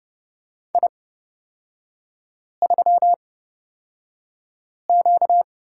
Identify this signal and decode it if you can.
Morse code, heard as a keyed tone, about 31 wpm, spaced as for 5 wpm, 690 Hz. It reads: I3Q
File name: course / 30:10wpm / 236 Lesson 38e - Letter q - Three Characters 30wpm